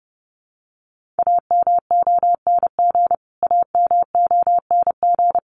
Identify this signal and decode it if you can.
Morse code, heard as a keyed tone, about 30 wpm, 700 Hz. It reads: AMONG AMONG
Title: Top 500 Words Repeated 30wpm